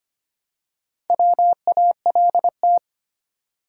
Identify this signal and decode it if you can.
Morse code, heard as a keyed tone, about 25 wpm, 700 Hz. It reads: WALT
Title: USA Common Names - Max 4 Letters - 25wpm